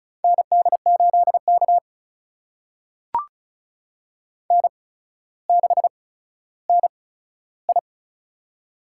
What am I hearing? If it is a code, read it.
Morse code, heard as a keyed tone, about 34 wpm, spaced as for 15 wpm, 700 Hz. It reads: ND8K N6NI
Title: USA Call Signs - 4 Characters - 8x Character Spacing - 35wpm